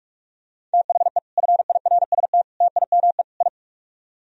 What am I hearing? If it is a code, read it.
Morse code, heard as a keyed tone, about 45 wpm, 700 Hz. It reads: THE FIRST TIME I